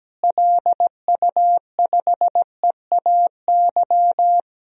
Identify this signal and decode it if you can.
Morse code, heard as a keyed tone, about 17 wpm, 700 Hz. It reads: LU5EAY
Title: DXCC-100 Call Signs - 6 Characters - Repeated 3x - 17wpm